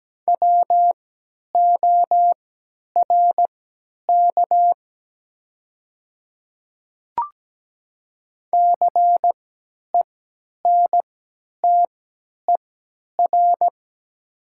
Morse code, heard as a keyed tone, about 17 wpm, spaced as for 10 wpm, 690 Hz. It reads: WORK CENTER
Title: 3x Character Spacing - Top 500 Words - 17wpm